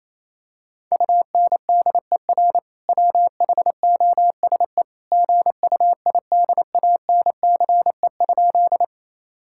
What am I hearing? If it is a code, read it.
Morse code, heard as a keyed tone, about 28 wpm, 700 Hz. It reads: UNDER WHOSE GUIDANCE?
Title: Sentences from Top 1600 Words 28wpm